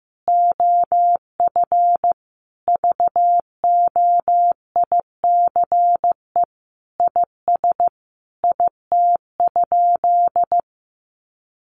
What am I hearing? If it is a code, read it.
Morse code, heard as a keyed tone, about 15 wpm, 700 Hz. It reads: OF VOICE IS IT?
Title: Sentences from Top 500 Words 15wpm